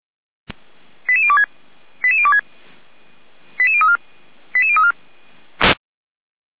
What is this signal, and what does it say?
Five-tone selective call (ZVEI), heard as quick runs of five tones, tones of about 70 ms. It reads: 88026 88026 88034 88034